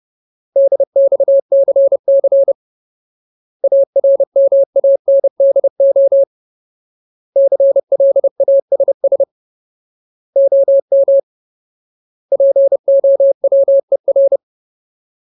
Morse code, heard as a keyed tone, about 30 wpm, 550 Hz. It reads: DXCC ARMANDO CLASS OM POWER
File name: Ultimate Word List / 30WPM Ultimate